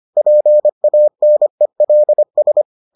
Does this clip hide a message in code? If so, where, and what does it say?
Yes: Morse code from 0.2 to 2.6 s, heard as a keyed tone, about 25 wpm, 600 Hz. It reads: PANELS